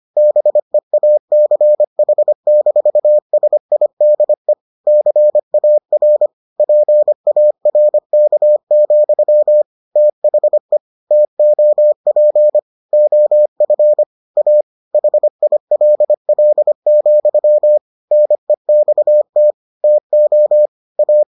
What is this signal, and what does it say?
Morse code, heard as a keyed tone, about 25 wpm, 600 Hz. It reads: BEACH-SIDE CAR PARK, THE TOP OF A HILL, NEXT TO A